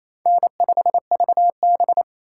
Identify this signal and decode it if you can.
Morse code, heard as a keyed tone, about 28 wpm, 700 Hz. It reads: N5VB
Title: USA Call Signs - 4 Characters - 3x Character Spacing - 28wpm